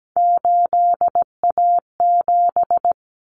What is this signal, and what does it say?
Morse code, heard as a keyed tone, about 17 wpm, 700 Hz. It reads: 8A7
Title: Common Cut Numbers - 3-Digit 17wpm